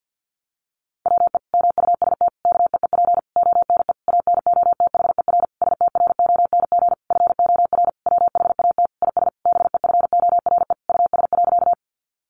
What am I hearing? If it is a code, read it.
Morse code, heard as a keyed tone, about 48 wpm, 700 Hz. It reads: WE MUST KEEP ONE ANOTHER STRONG FOR WHAT IS BEFORE US.
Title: Sentences from Top 600 Words 50wpm